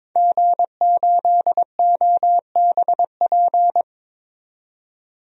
Morse code, heard as a keyed tone, about 22 wpm, 700 Hz. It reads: G8OBP
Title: DXCC-100 Call Signs - 5 Characters - 22wpm